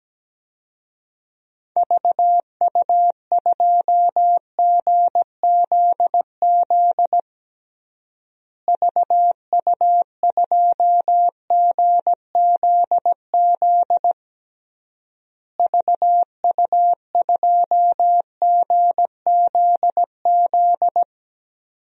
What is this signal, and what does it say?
Morse code, heard as a keyed tone, about 17 wpm, 700 Hz. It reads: VU2GZZ VU2GZZ VU2GZZ
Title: DXCC-100 Call Signs - 6 Characters - Repeated 3x - 17wpm